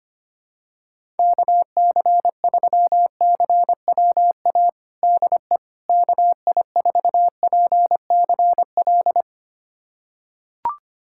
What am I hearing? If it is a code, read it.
Morse code, heard as a keyed tone, about 25 wpm, 700 Hz. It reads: KC3CWA DE KI4PCL
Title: USA Callsign DE Callsign 25wpm